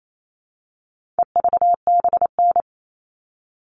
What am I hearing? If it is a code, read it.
Morse code, heard as a keyed tone, about 28 wpm, 700 Hz. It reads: EVBN
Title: All Cut Numbers - 4-Digit 28wpm